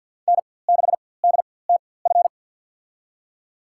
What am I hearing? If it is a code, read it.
Morse code, heard as a keyed tone, about 50 wpm, spaced as for 23 wpm, 700 Hz. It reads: N6DTF
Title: USA Call Signs - 5 Characters - 4x Character Spacing - 50wpm